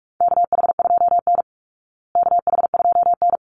Morse code, heard as a keyed tone, about 44 wpm, 700 Hz. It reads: KH2N KH2N
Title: Call Signs Repeated 3x - 45wpm